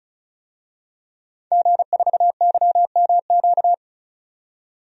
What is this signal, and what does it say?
Morse code, heard as a keyed tone, about 36 wpm, 700 Hz. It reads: G4YMQ
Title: DXCC-100 Call Signs - 5 Characters - 35wpm